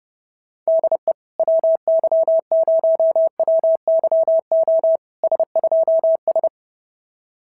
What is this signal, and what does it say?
Morse code, heard as a keyed tone, about 27 wpm, 650 Hz. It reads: DE WY0WYO S2S
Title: SOTA QSO Practice 30wpm